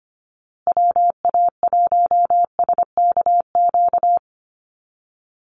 Morse code, heard as a keyed tone, about 25 wpm, 700 Hz. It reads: WA1SKQ
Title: Call Signs Repeated 2x - 25wpm